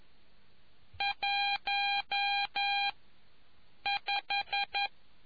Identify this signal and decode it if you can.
Morse code, heard as a keyed tone, about 11 wpm, 800 Hz. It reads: 1 5